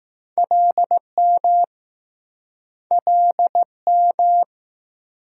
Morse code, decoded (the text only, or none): LM LM